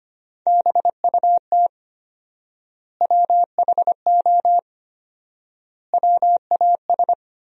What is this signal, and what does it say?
Morse code, heard as a keyed tone, about 25 wpm, 700 Hz. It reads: BUT WHO WAS
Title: 4x Word-Spacing - Sentences from Top 100 Words - 25wpm